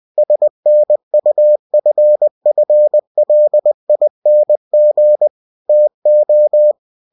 Morse code, heard as a keyed tone, about 20 wpm, 600 Hz. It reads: SNUFFLING TO
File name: cw-books / winnie-the-pooh / winnie-20wpm-page-0057